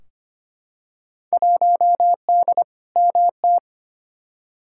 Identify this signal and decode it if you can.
Morse code, heard as a keyed tone, about 25 wpm, 700 Hz. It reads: 1D MT
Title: Field Day Exchanges - 25wpm